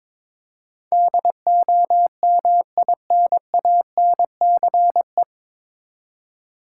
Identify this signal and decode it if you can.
Morse code, heard as a keyed tone, about 22 wpm, 700 Hz. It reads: DOMINANCE
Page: Morse Code Ninja practice 22wpm